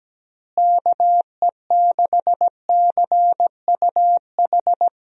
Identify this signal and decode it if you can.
Morse code, heard as a keyed tone, about 17 wpm, 700 Hz. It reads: KE6CUH
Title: USA Call Signs - 6 Characters - Repeated 3x - 17wpm